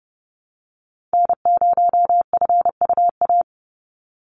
Morse code, heard as a keyed tone, about 30 wpm, 700 Hz. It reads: N0FUA
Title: USA Call Signs 30wpm